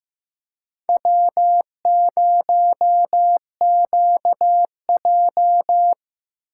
Morse code, heard as a keyed tone, about 15 wpm, 700 Hz. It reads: W0QJ
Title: USA Call Signs - 4 Characters - 8x Character Spacing - 15wpm